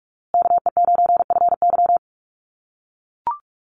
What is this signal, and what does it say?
Morse code, heard as a keyed tone, about 44 wpm, 700 Hz. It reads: KE9FY
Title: USA Call Signs Repeated 2x 45wpm